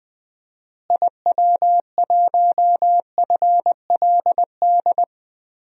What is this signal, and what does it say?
Morse code, heard as a keyed tone, about 20 wpm, 700 Hz. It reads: IW1FLD
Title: DXCC-100 Call Signs - 6 Characters - Repeated 3x - 20wpm